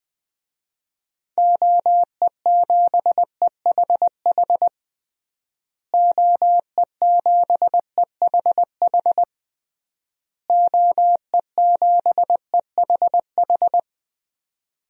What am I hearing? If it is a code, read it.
Morse code, heard as a keyed tone, about 20 wpm, 700 Hz. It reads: OE7EHH OE7EHH OE7EHH
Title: DXCC-100 Call Signs - 6 Characters - Repeated 3x - 20wpm